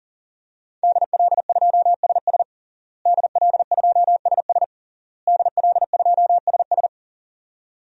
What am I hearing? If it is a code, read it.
Morse code, heard as a keyed tone, about 40 wpm, 700 Hz. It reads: DL2SS DL2SS DL2SS
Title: DXCC-100 Call Signs - 5 Characters - Repeated 3x - 40wpm